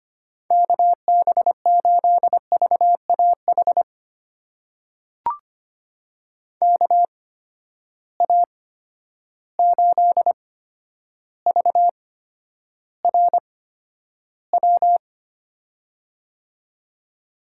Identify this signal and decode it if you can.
Morse code, heard as a keyed tone, about 25 wpm, spaced as for 11 wpm, 700 Hz. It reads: KB8VAH KA8VRW
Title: USA Call Signs - 6 Characters - 8x Character Spacing - 25wpm